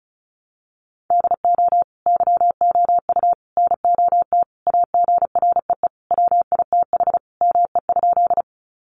Morse code, heard as a keyed tone, about 34 wpm, 700 Hz. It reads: DO YOU NOT AGREE WITH ME?
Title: Sentences from Top 1100 Words 35wpm